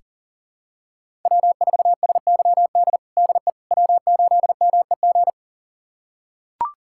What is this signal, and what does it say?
Morse code, heard as a keyed tone, about 40 wpm, 700 Hz. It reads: W4SYD DE W8MEG
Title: USA Callsign DE Callsign - 5 Characters - 40wpm